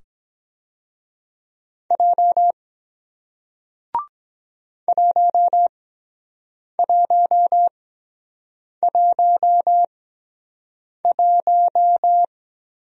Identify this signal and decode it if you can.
Morse code, heard as a keyed tone, about 23 wpm, spaced as for 7 wpm, 700 Hz. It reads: J 1111